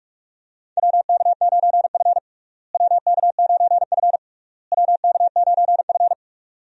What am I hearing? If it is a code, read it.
Morse code, heard as a keyed tone, about 46 wpm, 700 Hz. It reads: WK9F WK9F WK9F
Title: USA Call Signs - 4 Characters - Repeated 3x - 45wpm